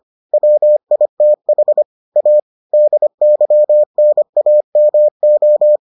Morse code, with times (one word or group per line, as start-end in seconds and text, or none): 0.3-1.8 WITH
2.2-2.4 A
2.7-5.8 DYNAMO